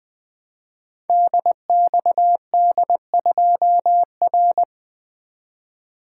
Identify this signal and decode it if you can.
Morse code, heard as a keyed tone, about 20 wpm, 700 Hz. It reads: DXD2R
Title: Code Groups - 5 Letter:Numbers - 20wpm